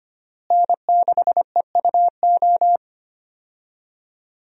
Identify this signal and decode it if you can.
Morse code, heard as a keyed tone, about 25 wpm, 700 Hz. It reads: N6EUO
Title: USA Call Signs - 5 Characters - Repeated 3x - 25wpm